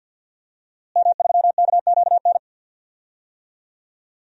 Morse code, heard as a keyed tone, about 46 wpm, 700 Hz. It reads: M3KQN
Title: DXCC-100 Call Signs - 5 Characters - Repeated 3x - 50wpm